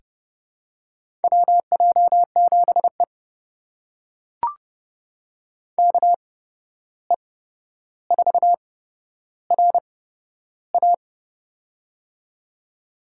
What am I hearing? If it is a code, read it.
Morse code, heard as a keyed tone, about 30 wpm, spaced as for 12 wpm, 700 Hz. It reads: WJ7E KE4RA
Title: USA Call Signs - 8x Character Spacing - 30wpm